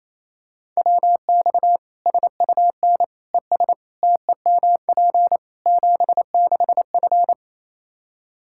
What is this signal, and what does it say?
Morse code, heard as a keyed tone, about 27 wpm, 700 Hz. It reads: WX SUN ES TEMP 76F
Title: Practice QSO - Exchange 2 - Info - 28wpm